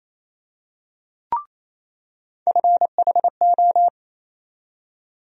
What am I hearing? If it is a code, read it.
Morse code, heard as a keyed tone, about 28 wpm, 700 Hz. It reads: FHO
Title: USA Call Sign Postfixes - 3 Letters 28wpm